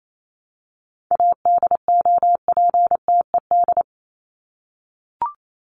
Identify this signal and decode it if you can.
Morse code, heard as a keyed tone, about 28 wpm, 700 Hz. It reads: ADOPTED
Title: ED Suffix - Top 100 Words with Base Words 28wpm